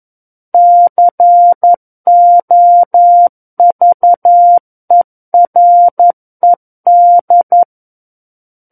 Morse code, heard as a keyed tone, about 11 wpm, 700 Hz. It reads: COVERED
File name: 11 WPM